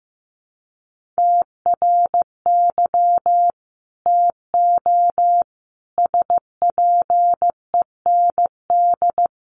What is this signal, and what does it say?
Morse code, heard as a keyed tone, about 15 wpm, 700 Hz. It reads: TRY TO SPEND
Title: Sentences from Top 800 Words 15wpm